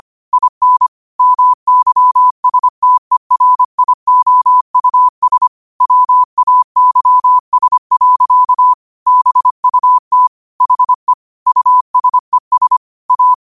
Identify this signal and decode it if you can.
Morse code, heard as a keyed tone, about 25 wpm, 1000 Hz. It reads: IN MYSTERIOUS WAYS. BUT HE USES A
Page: New quotes every day in morse code at 25 Words per minute.